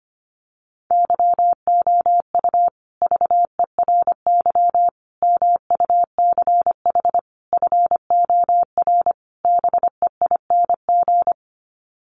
Morse code, heard as a keyed tone, about 25 wpm, 700 Hz. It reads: YOU VERY MUCH FOR BEING
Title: Sets of 5 Words Encore 25wpm